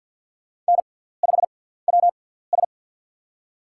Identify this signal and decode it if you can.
Morse code, heard as a keyed tone, about 52 wpm, spaced as for 17 wpm, 700 Hz. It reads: N5WS
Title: USA Call Signs - 4 Characters - 6x Character Spacing - 50wpm